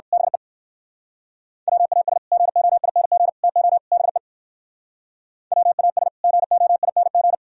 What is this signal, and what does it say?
Morse code, heard as a keyed tone, about 58 wpm, 700 Hz. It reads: BE WAS GOING TO BE WAS GOING